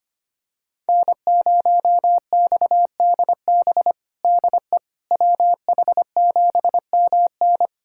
Morse code, heard as a keyed tone, about 25 wpm, 700 Hz. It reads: N0XDB DE WH7MN
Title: USA Callsign DE Callsign - 5 Characters - 25wpm